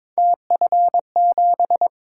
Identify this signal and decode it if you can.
Morse code, heard as a keyed tone, about 22 wpm, 700 Hz. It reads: TF7
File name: practice / 22wpm / Three Letter-Number Combinations - 22wpm